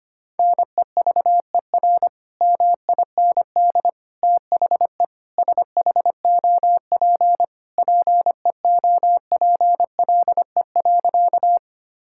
Morse code, heard as a keyed tone, about 25 wpm, 700 Hz. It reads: NEVER MIND THE SHOP PEOPLE.
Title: Sentences from Top 1600 Words 25wpm